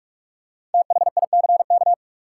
Morse code, heard as a keyed tone, about 42 wpm, 700 Hz. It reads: THICK
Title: Five Letter Words - 45wpm